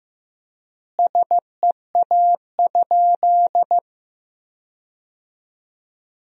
Morse code, heard as a keyed tone, about 16 wpm, 700 Hz. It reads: SEA?
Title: 3x Word-Spacing - Sentences from Top 500 Words - 15wpm